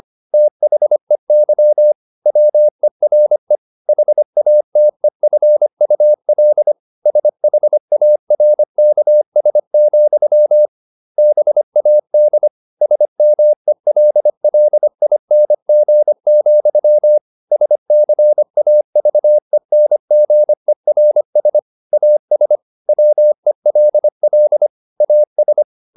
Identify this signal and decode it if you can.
Morse code, heard as a keyed tone, about 25 wpm, 600 Hz. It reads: THEY WERE HATEFUL SHARKS, BAD SMELLING, SCAVENGERS AS WELL AS